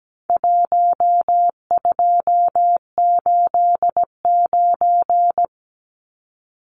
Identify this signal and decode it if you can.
Morse code, heard as a keyed tone, about 17 wpm, 700 Hz. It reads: 1289
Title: Numbers - Contest Style - 4-Digit 17wpm